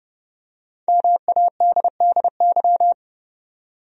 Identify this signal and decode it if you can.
Morse code, heard as a keyed tone, about 30 wpm, 700 Hz. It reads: MADDY
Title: USA Common Names - Max 5 Letters - 30wpm